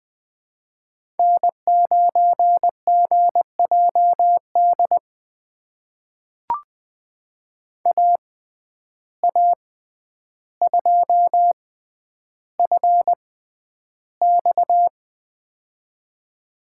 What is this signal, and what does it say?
Morse code, heard as a keyed tone, about 20 wpm, spaced as for 10 wpm, 700 Hz. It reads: N9GJD AA2FX
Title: USA Call Signs - 5 Characters - 6x Character Spacing - 20wpm